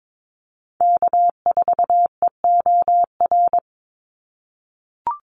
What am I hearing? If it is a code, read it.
Morse code, heard as a keyed tone, about 22 wpm, 700 Hz. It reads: K4EOR
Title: Call Signs - 22wpm